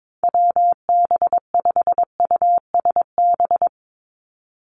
Morse code, heard as a keyed tone, about 22 wpm, 700 Hz. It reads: WB5USB
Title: NA SOTA Activator Callsigns 22wpm